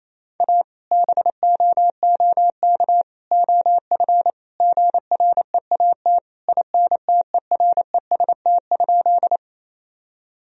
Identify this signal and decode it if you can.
Morse code, heard as a keyed tone, about 28 wpm, 700 Hz. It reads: A BOOK OF GREAT INTEREST?
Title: Sentences from Top 1000 Words - Review Part 2 - 28wpm